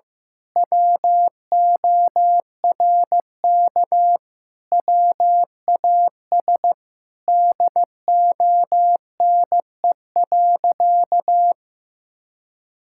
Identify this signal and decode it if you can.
Morse code, heard as a keyed tone, about 15 wpm, 700 Hz. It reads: WORK WAS DONE.
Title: Sentences from Top 1000 Words - Review Part 5 - 15wpm